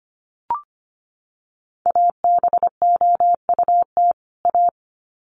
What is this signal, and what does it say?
Morse code, heard as a keyed tone, about 25 wpm, 700 Hz. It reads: ABOUT A
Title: Sets of 2 Words 25wpm